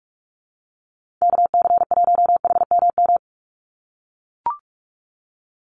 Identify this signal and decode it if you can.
Morse code, heard as a keyed tone, about 44 wpm, 700 Hz. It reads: KC1HMM
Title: USA Call Signs 45wpm